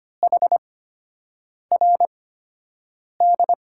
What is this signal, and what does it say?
Morse code, heard as a keyed tone, about 25 wpm, spaced as for 7 wpm, 700 Hz. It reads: HRD
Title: USA Call Signs - 8x Character Spacing - 25wpm